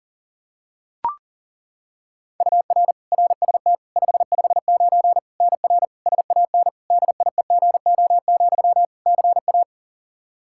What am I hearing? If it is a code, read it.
Morse code, heard as a keyed tone, about 37 wpm, 700 Hz. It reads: UR RST 559 NR SAN DIEGO, CA
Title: Practice QSO - Exchange 1 - Beginning - 40wpm